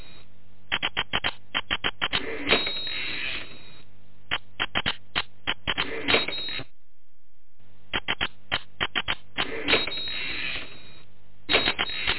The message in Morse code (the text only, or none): none